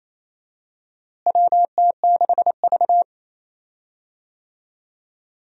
Morse code, heard as a keyed tone, about 28 wpm, 700 Hz. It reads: WT6V
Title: USA Call Signs - 4 Characters - Repeated 3x - 28wpm